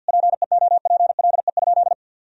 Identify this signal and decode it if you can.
Morse code, heard as a keyed tone, about 50 wpm, 700 Hz. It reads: PEOPLE?
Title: Sentences from Top 1000 Words - Review Part 3 - 50wpm